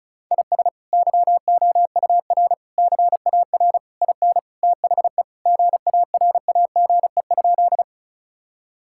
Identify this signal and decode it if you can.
Morse code, heard as a keyed tone, about 36 wpm, 700 Hz. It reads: IS YOUR CAR IN THE GARAGE?
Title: Sentences from Top 1600 Words 35wpm